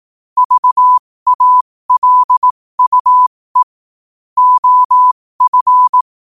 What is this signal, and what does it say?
Morse code, heard as a keyed tone, about 18 wpm, spaced as for 15 wpm, 1000 Hz. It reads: VALUE OF
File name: qotdfiles / 15wpm